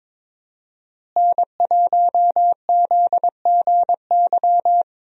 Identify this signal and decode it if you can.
Morse code, heard as a keyed tone, about 22 wpm, 700 Hz. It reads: N1ZGY
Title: USA Call Signs - 5 Characters - Repeated 2x - 22wpm